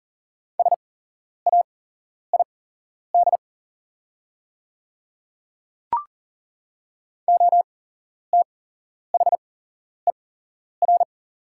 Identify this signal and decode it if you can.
Morse code, heard as a keyed tone, about 40 wpm, spaced as for 11 wpm, 700 Hz. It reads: SAID OTHER
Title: 8x Character Spacing - Top 100 Words - 40wpm